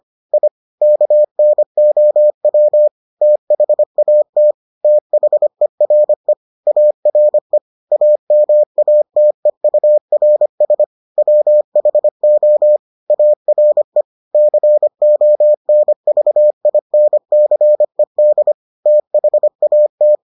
Morse code, heard as a keyed tone, about 25 wpm, 600 Hz. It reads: I KNOW THAT THERE ARE AMATEURS WHO ARE CONVINCED THAT